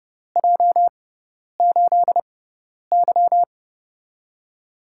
Morse code, heard as a keyed tone, about 30 wpm, spaced as for 10 wpm, 700 Hz. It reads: J8Y
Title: USA Call Signs - 4 Characters - 6x Character Spacing - 30wpm